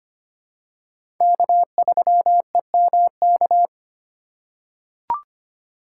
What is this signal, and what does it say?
Morse code, heard as a keyed tone, about 25 wpm, 700 Hz. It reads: K3EMK
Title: USA Call Signs - 5 Characters - 4x Character Spacing - 25wpm